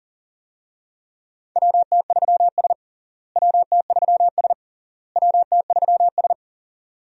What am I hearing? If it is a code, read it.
Morse code, heard as a keyed tone, about 40 wpm, 700 Hz. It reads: WT3S WT3S WT3S
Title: USA Call Signs - 4 Characters - Repeated 3x - 40wpm